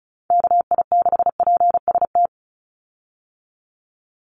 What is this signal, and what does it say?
Morse code, heard as a keyed tone, about 34 wpm, 700 Hz. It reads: KI6PST
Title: USA Call Signs 35wpm